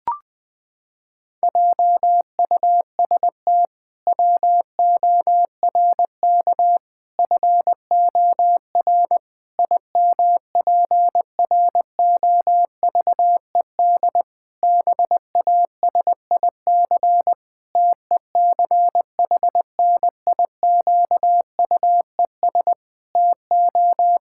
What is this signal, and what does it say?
Morse code, heard as a keyed tone, about 20 wpm, 700 Hz. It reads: JUST WORK FOR IMPROVED BASIC TECHNIQUES TO
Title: Pangram Challenge v2 - 20wpm